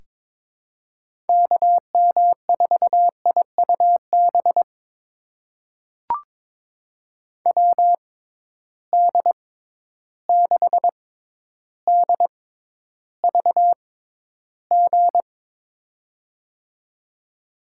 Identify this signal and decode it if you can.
Morse code, heard as a keyed tone, about 22 wpm, spaced as for 11 wpm, 700 Hz. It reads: KM4IUB WD6DVG
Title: USA Call Signs - 6 Characters - 6x Character Spacing - 22wpm